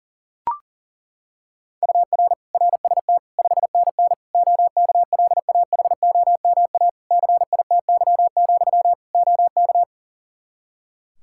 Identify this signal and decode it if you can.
Morse code, heard as a keyed tone, about 37 wpm, 700 Hz. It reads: UR RST 5NN OKLAHOMA CITY, OK
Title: Practice QSO - Exchange 1 - Beginning - 40wpm